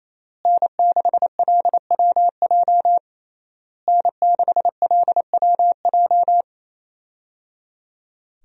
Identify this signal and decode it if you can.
Morse code, heard as a keyed tone, about 28 wpm, 700 Hz. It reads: N6LWJ N6LWJ
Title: USA Call Signs - 5 Characters - Repeated 3x - 28wpm